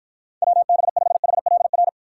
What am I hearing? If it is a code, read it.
Morse code, heard as a keyed tone, about 53 wpm, 700 Hz. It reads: WB5HLR